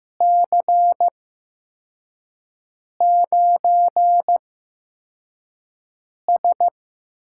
Morse code, heard as a keyed tone, about 15 wpm, spaced as for 4 wpm, 700 Hz. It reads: C9S
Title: USA Call Signs - 6 Characters - 8x Character Spacing - 15wpm